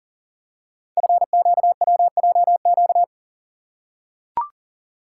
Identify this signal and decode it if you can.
Morse code, heard as a keyed tone, about 40 wpm, 700 Hz. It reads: FQWJQ
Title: Code Groups - 5 Letters - 40wpm